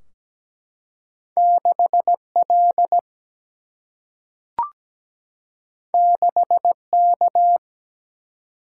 Morse code, heard as a keyed tone, about 17 wpm, 700 Hz. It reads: 6L 6K
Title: Single-Letter Single-Number 17wpm